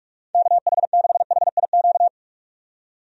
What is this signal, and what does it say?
Morse code, heard as a keyed tone, about 46 wpm, 700 Hz. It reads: KH6HIQ
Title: USA Call Signs - 6 Characters - 45wpm